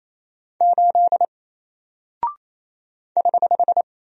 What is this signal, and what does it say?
Morse code, heard as a keyed tone, about 27 wpm, 700 Hz. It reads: 8 <HH>
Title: Single Character (Letter:Number:Punctuation:Prosign) Alt 28wpm